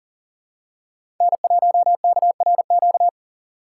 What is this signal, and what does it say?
Morse code, heard as a keyed tone, about 40 wpm, 700 Hz. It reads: N1KRQ
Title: USA Call Signs - 5 Characters - 2x Character Spacing - 40wpm